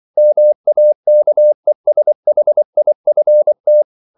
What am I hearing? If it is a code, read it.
Morse code, heard as a keyed tone, about 24 wpm, 600 Hz. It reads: MAKESHIFT